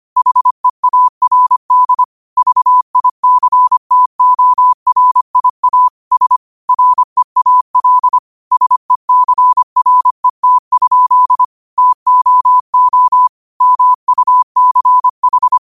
Morse code, heard as a keyed tone, about 25 wpm, 1000 Hz. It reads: HEARD VICTORIA S REAL SECRET? TOO MUCH
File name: qotdfiles / 25wpm